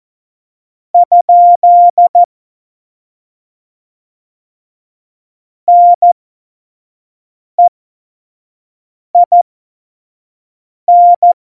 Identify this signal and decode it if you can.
Morse code, heard as a keyed tone, about 14 wpm, spaced as for 5 wpm, 700 Hz. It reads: ? NEIN
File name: Morsekurs / Witze / Witz 22. Mai 2018, 25 F70